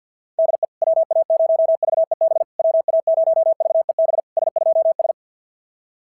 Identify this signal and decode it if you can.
Morse code, heard as a keyed tone, about 50 wpm, 650 Hz. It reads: DE WA0VEB WA0VEB S2S
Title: SOTA QSO Practice 50wpm